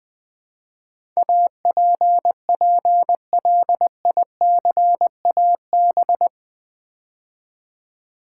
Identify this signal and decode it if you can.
Morse code, heard as a keyed tone, about 20 wpm, 700 Hz. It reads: APPLICAB
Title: Build A Word - 4 Letters At A Time - 10-Letter Words - 20wpm